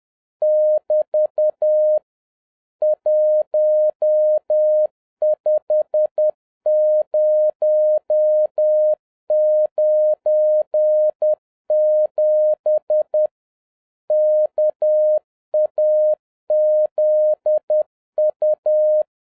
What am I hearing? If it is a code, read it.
Morse code, heard as a keyed tone, about 10 wpm, 610 Hz. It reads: <BT> 15097 KAZU